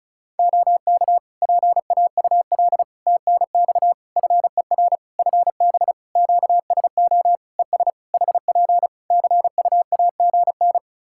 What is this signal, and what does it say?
Morse code, heard as a keyed tone, about 36 wpm, 700 Hz. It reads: OK PAUL TNX FER FB QSO ES HP CUAGN
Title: K1USN SST QSO Practice 35wpm